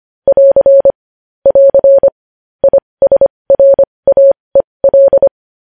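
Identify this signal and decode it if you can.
Morse code, heard as a keyed tone, about 25 wpm, spaced as for 20 wpm, 550 Hz. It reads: <AR> <AR> ISRAEL
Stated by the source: Morse code news headlines sent at 20.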